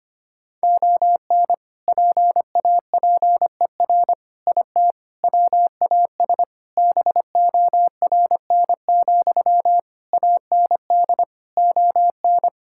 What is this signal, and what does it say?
Morse code, heard as a keyed tone, about 25 wpm, 700 Hz. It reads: ON PAPER IT WAS BORN, AND ON
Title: Sentences from Top 1600 Words 25wpm